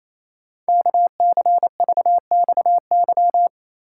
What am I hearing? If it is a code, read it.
Morse code, heard as a keyed tone, about 29 wpm, 700 Hz. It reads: KCVXY
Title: Code Groups - 5 Letters - 28wpm